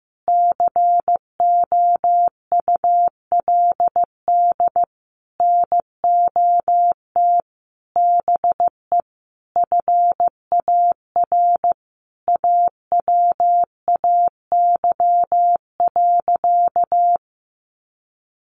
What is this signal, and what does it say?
Morse code, heard as a keyed tone, about 15 wpm, 700 Hz. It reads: COULD NOT BE FAR AWAY.
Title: Sentences from Top 1100 Words 15wpm